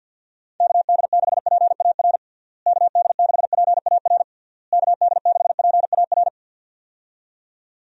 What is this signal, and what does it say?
Morse code, heard as a keyed tone, about 46 wpm, 700 Hz. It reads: KD6PAR KD6PAR KD6PAR
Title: USA Call Signs - 6 Characters - Repeated 3x - 50wpm